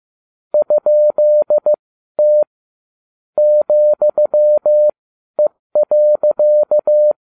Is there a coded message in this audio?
Yes: Morse code from 0.5 to 7.1 s, heard as a keyed tone, about 15 wpm, spaced as for 12 wpm, 600 Hz. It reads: ?T ,E.